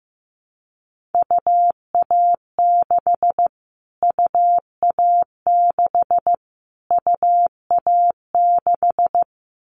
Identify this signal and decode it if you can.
Morse code, heard as a keyed tone, about 15 wpm, 700 Hz. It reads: UA6 UA6 UA6
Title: Callsign Prefixes for Top 25 DXCC - Repeated 3x - 15wpm